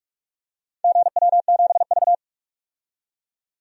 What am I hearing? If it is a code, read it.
Morse code, heard as a keyed tone, about 41 wpm, 700 Hz. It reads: GW7V
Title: DXCC-100 Call Signs - 4 Characters - 45wpm